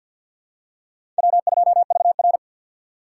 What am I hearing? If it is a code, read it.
Morse code, heard as a keyed tone, about 48 wpm, 700 Hz. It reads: W2VR
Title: USA Call Signs - 4x Character Spacing - 50wpm